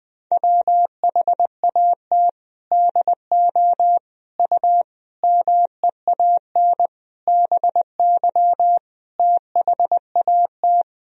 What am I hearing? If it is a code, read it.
Morse code, heard as a keyed tone, about 21 wpm, 700 Hz. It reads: WHAT DO U MEAN BY THAT
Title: Sentences from Top 400 Words and Abbreviations 20wpm